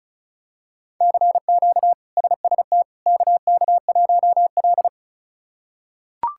Morse code, heard as a keyed tone, about 36 wpm, 700 Hz. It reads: CQ SST KK1L
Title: K1USN SST QSO Practice - 35wpm